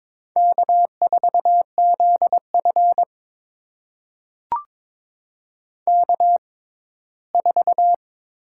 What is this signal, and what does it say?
Morse code, heard as a keyed tone, about 22 wpm, spaced as for 15 wpm, 700 Hz. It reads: K4ZF K4
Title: USA Call Signs - 4 Characters - 6x Character Spacing - 22wpm